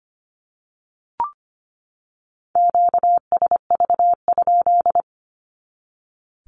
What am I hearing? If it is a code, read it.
Morse code, heard as a keyed tone, about 25 wpm, 700 Hz. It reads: QSV?
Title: Q-Codes 25wpm